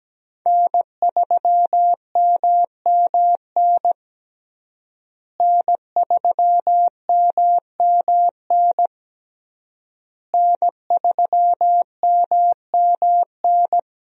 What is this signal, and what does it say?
Morse code, heard as a keyed tone, about 17 wpm, 700 Hz. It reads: N3MMN N3MMN N3MMN
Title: USA Call Signs - 5 Characters - Repeated 3x - 17wpm